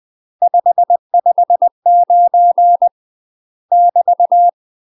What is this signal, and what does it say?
Morse code, heard as a keyed tone, about 20 wpm, 700 Hz. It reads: 559 <BT>
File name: UK MASTER QSOs TRAINING / 20WPM / UK MASTER QSOs 0046